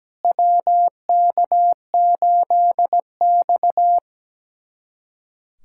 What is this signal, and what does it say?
Morse code, heard as a keyed tone, about 17 wpm, 700 Hz. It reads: WK8X
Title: USA Callsign DE Callsign - 4 Characters - 17wpm